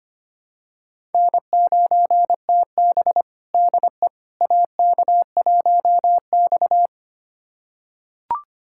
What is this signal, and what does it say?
Morse code, heard as a keyed tone, about 25 wpm, 700 Hz. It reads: N9TB DE AK1X
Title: USA Callsign DE Callsign - 4 Characters - 25wpm